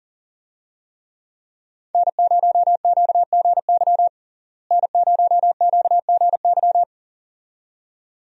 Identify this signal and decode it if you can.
Morse code, heard as a keyed tone, about 40 wpm, 700 Hz. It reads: N0QGY N0QGY
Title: USA Call Signs - 5 Characters - Repeated 2x - 40wpm